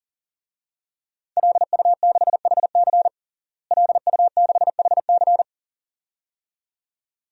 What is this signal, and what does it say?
Morse code, heard as a keyed tone, about 40 wpm, 700 Hz. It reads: LU6HC LU6HC
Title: DXCC-100 Call Signs - 5 Characters - Repeated 2x - 40wpm